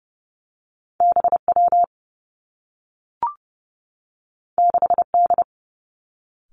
Morse code, heard as a keyed tone, about 30 wpm, 700 Hz. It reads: BW 6D
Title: Two Challenging Letter-Number Combinations 30wpm